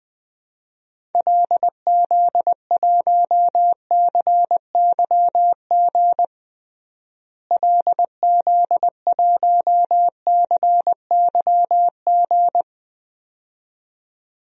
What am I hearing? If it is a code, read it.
Morse code, heard as a keyed tone, about 20 wpm, 700 Hz. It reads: LZ1CYG LZ1CYG
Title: DXCC-100 Call Signs - 6 Characters - Repeated 2x - 20wpm